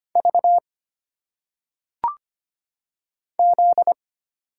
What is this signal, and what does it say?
Morse code, heard as a keyed tone, about 25 wpm, 700 Hz. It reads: V Z
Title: Single Letters - Phonetic Pronunciation 25wpm